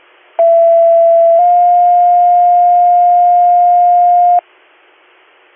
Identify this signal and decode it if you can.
Two-tone page, heard as a one-second tone, then a three-second tone, about 670 then 710 Hz. It is heard from 0.4 to 4.4 s.